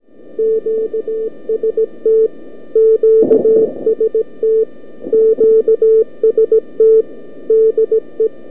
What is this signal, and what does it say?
Morse code, heard as a keyed tone, about 17 wpm, 440 Hz. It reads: QST QST QST DE
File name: k5cm-80m-by-abi-jan-2011